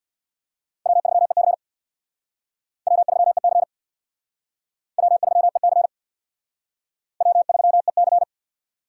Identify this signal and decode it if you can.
Morse code, heard as a keyed tone, about 62 wpm, 700 Hz. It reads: W3EC W3EC W3EC W3EC
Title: Speed Racing - USA 1x2 Call Signs 50wpm